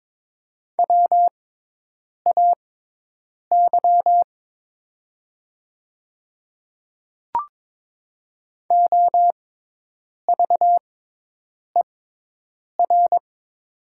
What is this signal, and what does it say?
Morse code, heard as a keyed tone, about 22 wpm, spaced as for 8 wpm, 700 Hz. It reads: WAY OVER
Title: 6x Character Spacing - Top 100 Words - 22wpm